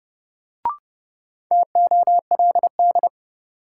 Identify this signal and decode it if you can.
Morse code, heard as a keyed tone, about 30 wpm, 700 Hz. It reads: TOLD
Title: Sentences from Top 400 Words - New Word Drill - 30wpm